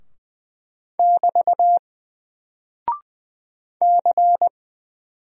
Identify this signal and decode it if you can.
Morse code, heard as a keyed tone, about 20 wpm, 700 Hz. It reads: <BT> C
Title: Single Character (Letter:Number:Punctuation:Prosign) Alt 20wpm